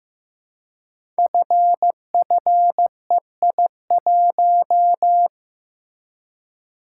Morse code, heard as a keyed tone, about 15 wpm, 700 Hz. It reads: FFEI1
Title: Code Groups - 5 Letter:Numbers - 15wpm